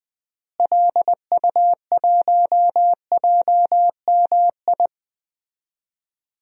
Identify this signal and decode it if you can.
Morse code, heard as a keyed tone, about 20 wpm, 700 Hz. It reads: LU1JMI
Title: DXCC-100 Call Signs - 6 Characters - Repeated 2x - 20wpm